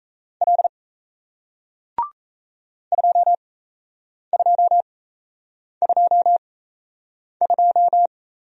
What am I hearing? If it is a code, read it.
Morse code, heard as a keyed tone, about 37 wpm, spaced as for 8 wpm, 700 Hz. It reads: L 2222